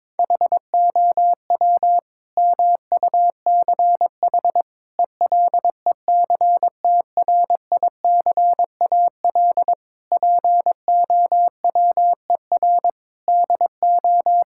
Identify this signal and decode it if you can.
Morse code, heard as a keyed tone, about 22 wpm, 700 Hz. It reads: HOW MUCH ELECTRICAL POWER DO
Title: Sentences from Top 1000 Words - Review Part 6 - 22wpm